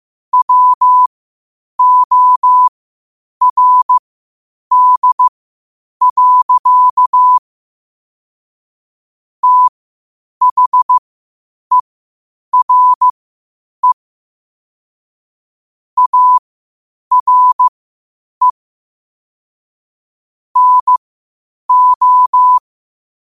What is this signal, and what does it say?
Morse code, heard as a keyed tone, about 15 wpm, spaced as for 8 wpm, 1000 Hz. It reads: WORD. THERE ARE NO
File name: qotdfiles / 8wpm